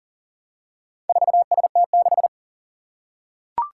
Morse code, heard as a keyed tone, about 40 wpm, 700 Hz. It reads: 4ST6
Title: Four Challenging Letter-Number Combinations 40wpm